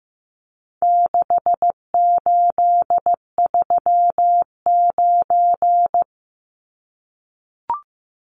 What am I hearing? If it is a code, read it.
Morse code, heard as a keyed tone, about 15 wpm, 700 Hz. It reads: 6839
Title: Numbers - 4-Digit 15wpm